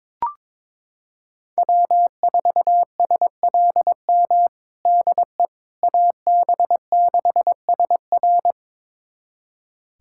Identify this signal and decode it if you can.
Morse code, heard as a keyed tone, about 22 wpm, 700 Hz. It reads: W4SLM DE AB6SR
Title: USA Callsign DE Callsign - 5 Characters - 22wpm